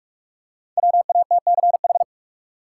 Morse code, heard as a keyed tone, about 45 wpm, 700 Hz. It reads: WATCH
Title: Five Letter Words - 45wpm